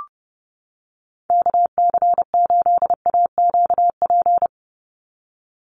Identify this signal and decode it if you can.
Morse code, heard as a keyed tone, about 30 wpm, 700 Hz. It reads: KC8AQP
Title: USA Call Signs 30wpm